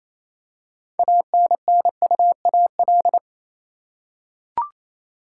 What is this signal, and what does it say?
Morse code, heard as a keyed tone, about 28 wpm, 700 Hz. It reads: ANNUAL
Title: Six Letter Words - 28wpm